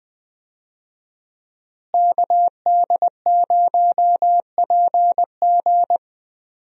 Morse code, heard as a keyed tone, about 20 wpm, 700 Hz. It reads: KD0PG
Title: USA Call Signs - 5 Characters - Repeated 3x - 20wpm